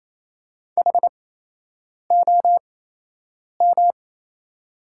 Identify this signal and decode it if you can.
Morse code, heard as a keyed tone, about 28 wpm, spaced as for 8 wpm, 700 Hz. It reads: HOM
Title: 8x Character Spacing - Top 200 Words - 28wpm